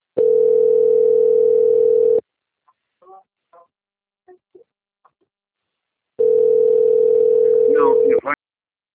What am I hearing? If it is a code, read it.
Ringback tone (North America), 440 + 480 Hz, with speech.
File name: out- - - - - .4442